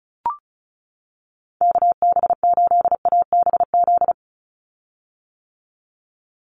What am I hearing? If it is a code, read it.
Morse code, heard as a keyed tone, about 34 wpm, 700 Hz. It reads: KB8ABZ
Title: USA Call Signs 35wpm